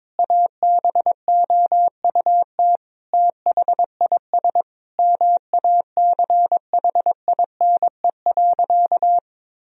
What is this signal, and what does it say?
Morse code, heard as a keyed tone, about 22 wpm, 700 Hz. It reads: ABOUT THIS MACHINE.
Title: Sentences from Top 1000 Words - Review Part 2 - 22wpm